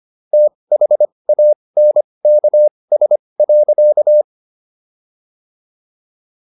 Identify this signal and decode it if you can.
Morse code, heard as a keyed tone, about 25 wpm, spaced as for 20 wpm, 600 Hz. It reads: THANKS.